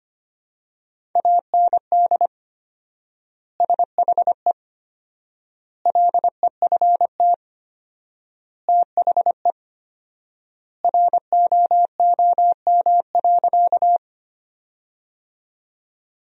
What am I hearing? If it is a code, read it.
Morse code, heard as a keyed tone, about 25 wpm, 700 Hz. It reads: AND SHE LEFT THE ROOM.
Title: 4x Word-Spacing - Sentences from Top 300 Words - 25wpm